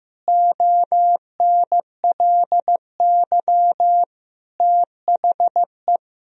Morse code, heard as a keyed tone, about 15 wpm, 700 Hz. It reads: ONLY THE
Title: Sentences from Top 1000 Words - Review Part 2 - 15wpm